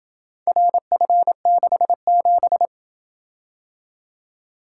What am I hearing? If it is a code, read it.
Morse code, heard as a keyed tone, about 27 wpm, 700 Hz. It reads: RF67